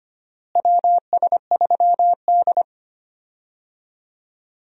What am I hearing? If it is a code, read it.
Morse code, heard as a keyed tone, about 25 wpm, 700 Hz. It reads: WS3D
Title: USA Call Signs - 4 Characters - Repeated 2x - 25wpm